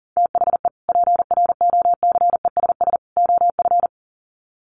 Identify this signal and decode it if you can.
Morse code, heard as a keyed tone, about 37 wpm, 700 Hz. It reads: THE PROCESS OF
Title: Sets of 3 Words 40wpm